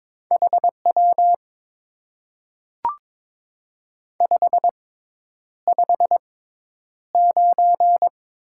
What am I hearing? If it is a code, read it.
Morse code, heard as a keyed tone, about 22 wpm, spaced as for 10 wpm, 700 Hz. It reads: HW 559
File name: Top 90 QSO Elements - 6x Character Spacing - 22wpm